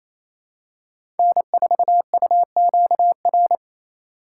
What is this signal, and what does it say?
Morse code, heard as a keyed tone, about 27 wpm, 700 Hz. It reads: N4UQR
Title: USA Call Signs - 4x Character Spacing - 28wpm